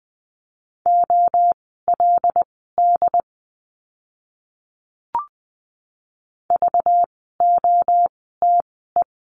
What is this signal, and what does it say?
Morse code, heard as a keyed tone, about 20 wpm, spaced as for 15 wpm, 700 Hz. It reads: OLD VOTE
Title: 192 Lesson 31f - Letter v - Words 20wpm